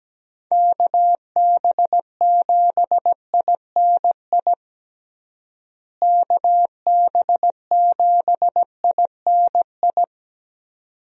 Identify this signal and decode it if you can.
Morse code, heard as a keyed tone, about 17 wpm, 700 Hz. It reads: KB7INI KB7INI
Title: USA Call Signs - 6 Characters - Repeated 2x - 17wpm